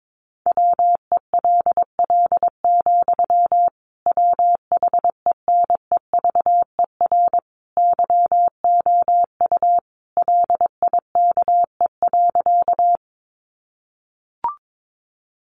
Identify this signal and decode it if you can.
Morse code, heard as a keyed tone, about 22 wpm, 700 Hz. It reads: WELL, WHENEVER YOU LIKE.
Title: Sentences from Top 1200 Words 22wpm